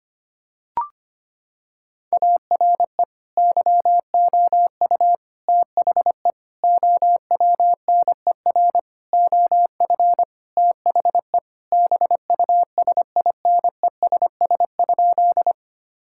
Morse code, heard as a keyed tone, about 25 wpm, 700 Hz. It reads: ARE YOU THE OWNER OF THE BUSINESS?
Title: Sentences from Top 1600 Words 25wpm